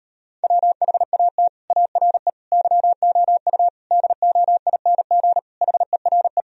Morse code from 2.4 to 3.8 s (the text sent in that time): YOU